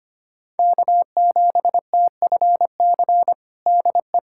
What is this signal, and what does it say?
Morse code, heard as a keyed tone, about 25 wpm, 700 Hz. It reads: K7TFC DE
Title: USA Callsign DE Callsign 25wpm